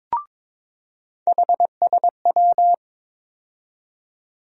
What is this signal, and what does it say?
Morse code, heard as a keyed tone, about 22 wpm, 700 Hz. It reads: HSW
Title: USA Call Sign Postfixes - 3 Letters 22wpm